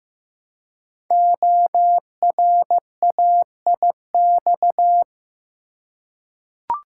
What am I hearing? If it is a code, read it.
Morse code, heard as a keyed tone, about 15 wpm, 700 Hz. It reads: ORAIX
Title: Code Groups - 5 Letters - 15wpm